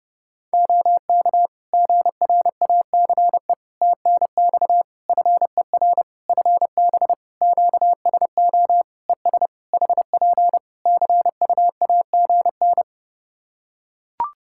Morse code, heard as a keyed tone, about 31 wpm, 700 Hz. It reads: OK GRACE TNX FER FB QSO ES HP CUAGN
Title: Practice QSO - Exchange 4 - Ending - 30wpm